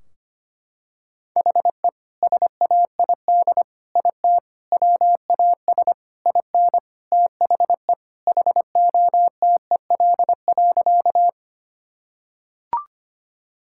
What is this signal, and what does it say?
Morse code, heard as a keyed tone, about 25 wpm, 700 Hz. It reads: HE SAID IT WAS IN THE HOTEL.
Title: Sentences from Top 1600 Words 25wpm